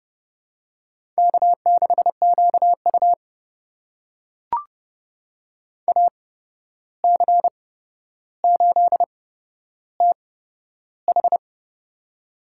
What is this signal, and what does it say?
Morse code, heard as a keyed tone, about 30 wpm, spaced as for 12 wpm, 700 Hz. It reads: K6QU AC8TH
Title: USA Call Signs - 8x Character Spacing - 30wpm